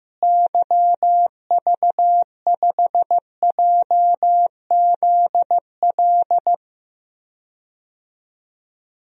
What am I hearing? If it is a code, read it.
Morse code, heard as a keyed tone, about 14 wpm, 700 Hz. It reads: YV5JZL
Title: DXCC-100 Call Signs - 6 Characters - Repeated 2x - 15wpm